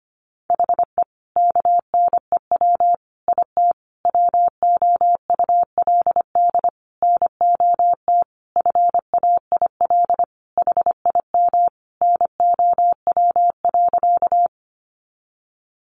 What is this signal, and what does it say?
Morse code, heard as a keyed tone, about 25 wpm, 700 Hz. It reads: HE KNEW IT WOULD NOT FAIL HIM NOW.
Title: Sentences from Top 1100 Words 25wpm